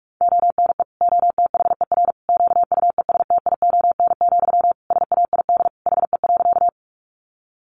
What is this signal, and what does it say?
Morse code, heard as a keyed tone, about 44 wpm, 700 Hz. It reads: ONE OTHER QUESTION, SAID HE.
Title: Sentences from Top 400 Words 45wpm